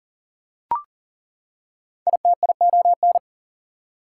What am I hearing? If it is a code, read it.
Morse code, heard as a keyed tone, about 40 wpm, 700 Hz. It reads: ITION
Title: Prefixes and Suffixes 40wpm